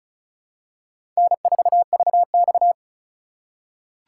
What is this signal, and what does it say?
Morse code, heard as a keyed tone, about 36 wpm, 700 Hz. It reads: N4VX
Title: USA Call Signs - 4 Characters - 3x Character Spacing - 35wpm